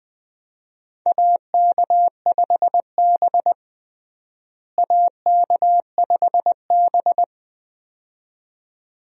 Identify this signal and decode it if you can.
Morse code, heard as a keyed tone, about 20 wpm, 700 Hz. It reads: AK5B AK5B
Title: USA Call Signs - 4 Characters - Repeated 2x - 20wpm